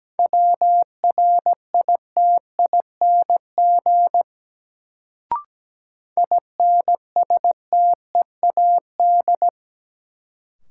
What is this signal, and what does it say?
Morse code, heard as a keyed tone, about 17 wpm, 700 Hz. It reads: WRITING INSTEAD
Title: Sentences from Top 800 Words - New Word Drill - 17wpm